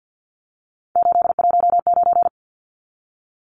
Morse code, heard as a keyed tone, about 50 wpm, 700 Hz. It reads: 819
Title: Numbers - 3-Digit 50wpm